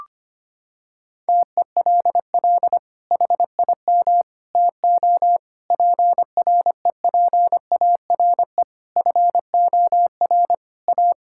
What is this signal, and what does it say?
Morse code, heard as a keyed tone, about 25 wpm, 700 Hz. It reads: TELL HIM TO PREPARE FOR A
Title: Sentences from Top 1000 Words - Review Part 6 - 25wpm